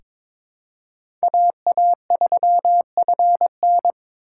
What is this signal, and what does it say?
Morse code, heard as a keyed tone, about 22 wpm, 700 Hz. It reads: AA3FN
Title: USA Call Signs - 5 Characters - 4x Character Spacing - 22wpm